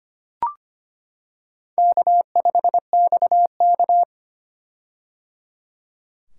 Morse code, heard as a keyed tone, about 25 wpm, 700 Hz. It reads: K5XK
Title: K1USN SST QSO Practice - 25wpm